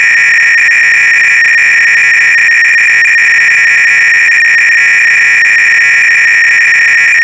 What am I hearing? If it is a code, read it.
300-baud modem data.